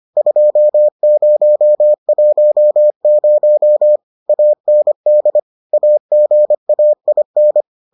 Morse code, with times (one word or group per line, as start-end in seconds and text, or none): 0.2-4.0 2010
4.3-5.4 AND
5.7-7.6 AGAIN